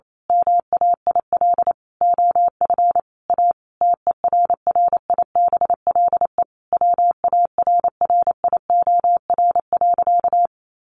Morse code, heard as a keyed tone, about 28 wpm, 700 Hz. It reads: MAIL OF A TERRIBLE WARRIOR.
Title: A Sound Of Thunder 28wpm